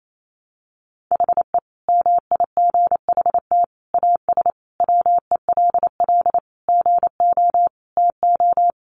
Morse code, heard as a keyed tone, about 28 wpm, 700 Hz. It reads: HE MIGHT AS WELL GO TO
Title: Sentences from Top 1300 Words 28wpm